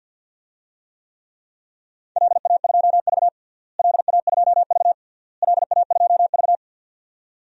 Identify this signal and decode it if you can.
Morse code, heard as a keyed tone, about 48 wpm, 700 Hz. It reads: LA2V LA2V LA2V
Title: DXCC-100 Call Signs - 4 Characters - Repeated 3x - 50wpm